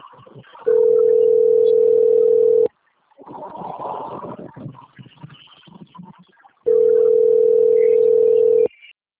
Ringback tone (North America), 440 + 480 Hz.